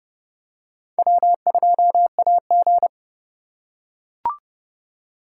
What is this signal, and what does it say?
Morse code, heard as a keyed tone, about 29 wpm, 700 Hz. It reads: W2AG